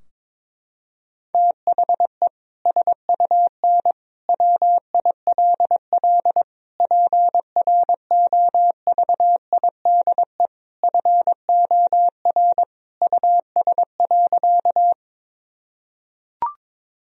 Morse code, heard as a keyed tone, about 22 wpm, 700 Hz. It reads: THE SUN WILL PROVIDE FOR US.
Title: Sentences from Top 1000 Words - Review Part 3 - 22wpm